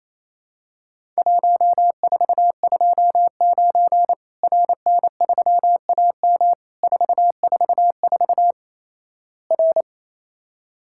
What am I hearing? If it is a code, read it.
Morse code, heard as a keyed tone, about 27 wpm, 700 Hz. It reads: 1429 RN3AM 444 R
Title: Worked All Europe DX Contest - QTC Exchange - 28wpm